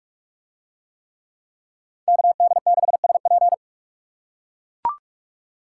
Morse code, heard as a keyed tone, about 40 wpm, 700 Hz. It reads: KD6SP